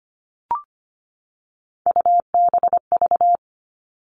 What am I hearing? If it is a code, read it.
Morse code, heard as a keyed tone, about 25 wpm, 700 Hz. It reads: UBV
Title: All Cut Numbers - 3-Digit 25wpm